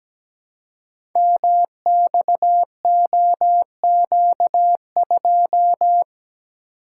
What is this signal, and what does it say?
Morse code, heard as a keyed tone, about 17 wpm, 700 Hz. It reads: MXOQ2
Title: Code Groups - 5 Letter:Numbers - 17wpm